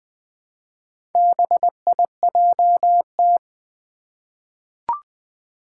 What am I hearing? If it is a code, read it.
Morse code, heard as a keyed tone, about 20 wpm, 700 Hz. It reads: BIJT
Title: Four Challenging Letter-Number Combinations 20wpm